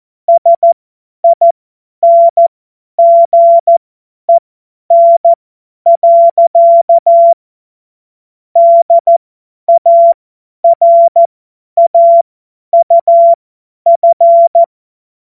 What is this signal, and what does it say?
Morse code, heard as a keyed tone, about 14 wpm, spaced as for 10 wpm, 680 Hz. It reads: SINGEN. DARAUF
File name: Morsekurs / Witze / Witz 12. Oktober 2021, 50 F70